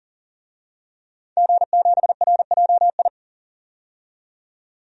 Morse code, heard as a keyed tone, about 40 wpm, 700 Hz. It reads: G7RJI
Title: DXCC-100 Call Signs - 5 Characters - 40wpm